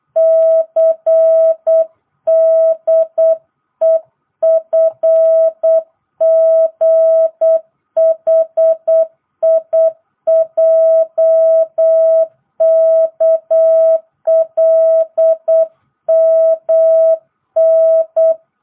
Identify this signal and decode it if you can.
Morse code, heard as a keyed tone, about 8 wpm, 650 Hz. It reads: CDEFGHIJKLMN